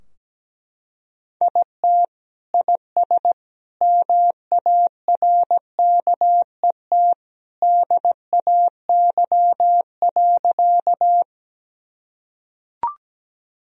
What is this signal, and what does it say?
Morse code, heard as a keyed tone, about 17 wpm, 700 Hz. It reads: IT IS MARKET DAY.
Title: Sentences from Top 1000 Words - Review Part 3 - 17wpm